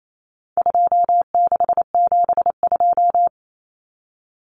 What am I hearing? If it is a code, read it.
Morse code, heard as a keyed tone, about 28 wpm, 700 Hz. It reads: 2672
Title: Numbers - 4-Digit 28wpm